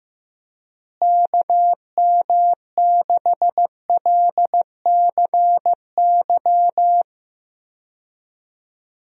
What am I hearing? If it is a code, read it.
Morse code, heard as a keyed tone, about 15 wpm, 700 Hz. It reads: KM6LCY
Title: USA Call Signs - 6 Characters - Repeated 2x - 15wpm